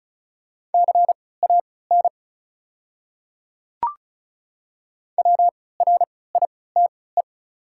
Morse code, heard as a keyed tone, about 36 wpm, spaced as for 20 wpm, 690 Hz. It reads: CAN WRITE
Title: 3x Character Spacing - Top 100 Words - 35wpm